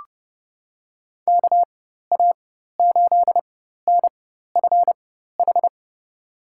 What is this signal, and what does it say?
Morse code, heard as a keyed tone, about 30 wpm, spaced as for 14 wpm, 700 Hz. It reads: KA8NFH
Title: USA Call Signs - 4x Character Spacing - 30wpm